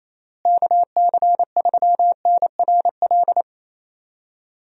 Morse code, heard as a keyed tone, about 28 wpm, 700 Hz. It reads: KC3NRL
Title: USA Call Signs - 6 Characters - Repeated 3x - 28wpm